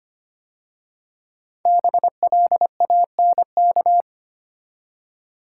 Morse code, heard as a keyed tone, about 25 wpm, 700 Hz. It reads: BLANK
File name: 5 Letter Words - First 3 Letters Given - 25wpm